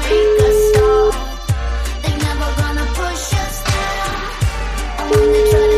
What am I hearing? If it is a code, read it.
Ringback tone (Europe, 425 Hz), with noise.